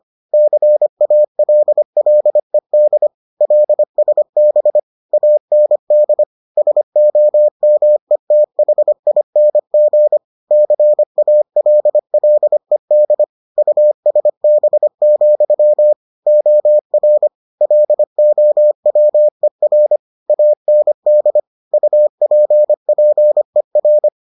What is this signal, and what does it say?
Morse code, heard as a keyed tone, about 25 wpm, 600 Hz. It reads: CALLED LSB AND SOMETHING CALLED USB, OR LOWER AND UPPER